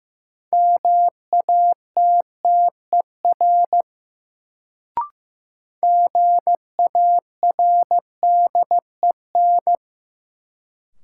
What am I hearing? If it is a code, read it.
Morse code, heard as a keyed tone, about 15 wpm, 700 Hz. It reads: MATTER GARDEN
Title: Sentences from Top 700 Words - New Word Drill - 15wpm